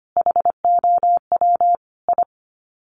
Morse code, heard as a keyed tone, about 25 wpm, 700 Hz. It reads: HOW I
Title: Sets of 5 Words Encore2 25wpm